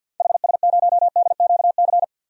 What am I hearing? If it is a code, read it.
Morse code, heard as a keyed tone, about 52 wpm, 700 Hz. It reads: HS0DQC
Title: DXCC-100 Call Signs - 6 Characters - Repeated 2x - 50wpm